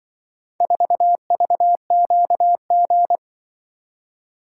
Morse code, heard as a keyed tone, about 24 wpm, 700 Hz. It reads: 4VQG
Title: Speed-Racing - Four Letter-Number Combinations - 20wpm